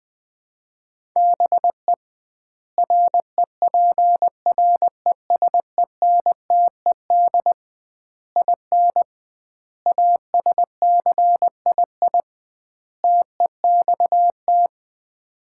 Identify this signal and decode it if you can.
Morse code, heard as a keyed tone, about 21 wpm, 700 Hz. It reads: BE REPRESENTED IN ASCII TEXT
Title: 2x Word-Spacing - Sets of 5 Words - 20wpm